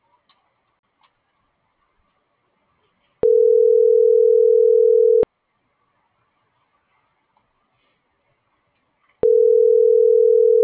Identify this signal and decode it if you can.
Ringback tone (North America), 440 + 480 Hz.